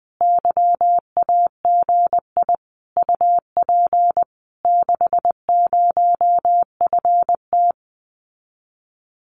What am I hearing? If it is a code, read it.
Morse code, heard as a keyed tone, about 20 wpm, 700 Hz. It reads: YAGI UP 60FT
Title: Common QSO Phrases Encore 20wpm